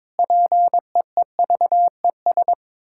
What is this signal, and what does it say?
Morse code, heard as a keyed tone, about 22 wpm, 700 Hz. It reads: PEEVES
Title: Pangram Challenge v2 - 22wpm